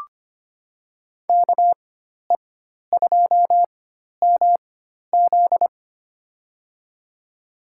Morse code, heard as a keyed tone, about 25 wpm, spaced as for 12 wpm, 700 Hz. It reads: KE2MZ
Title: USA Call Signs - 4x Character Spacing - 25wpm